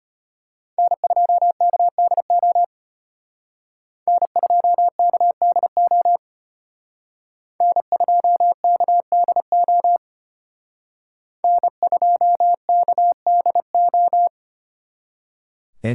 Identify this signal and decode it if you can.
Morse code, heard as a keyed tone, about 31 wpm, 700 Hz. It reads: N2KDO N2KDO N2KDO N2KDO
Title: Speed Racing - USA 1x3 Call Signs 25wpm